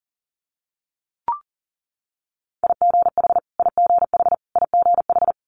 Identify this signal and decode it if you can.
Morse code, heard as a keyed tone, about 41 wpm, 700 Hz. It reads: IGH IGH IGH
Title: Multi-letter Phonemes Repeated 3x 40wpm